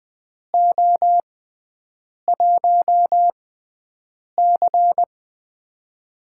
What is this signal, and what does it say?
Morse code, heard as a keyed tone, about 20 wpm, spaced as for 7 wpm, 700 Hz. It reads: O1C
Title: USA Call Signs - 4 Characters - 6x Character Spacing - 20wpm